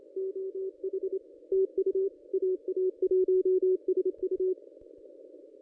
Morse code, heard as a keyed tone, about 26 wpm, 380 Hz. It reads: OH TU AA1SU
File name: contests / ss-cw-k5tr-op-n5rz -2200